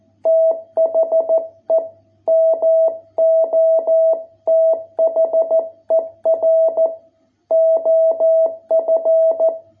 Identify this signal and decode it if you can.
Morse code, heard as a keyed tone, about 14 wpm, 660 Hz. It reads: THE MOTHER OF